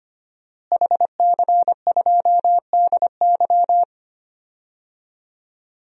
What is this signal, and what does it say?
Morse code, heard as a keyed tone, about 25 wpm, 700 Hz. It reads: HC2DY
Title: DXCC-100 Call Signs - 5 Characters - Repeated 3x - 25wpm